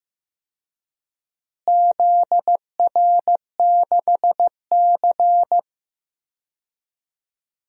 Morse code, heard as a keyed tone, about 16 wpm, 700 Hz. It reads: ZR6C
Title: DXCC-100 Call Signs - 4 Characters - Repeated 2x - 15wpm